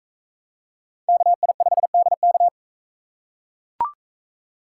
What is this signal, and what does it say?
Morse code, heard as a keyed tone, about 41 wpm, 700 Hz. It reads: KI5DK